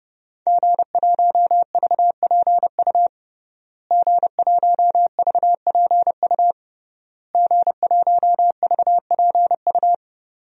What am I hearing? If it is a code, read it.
Morse code, heard as a keyed tone, about 29 wpm, 700 Hz. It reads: G1VPU G1VPU G1VPU
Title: DXCC-100 Call Signs - 5 Characters - Repeated 3x - 30wpm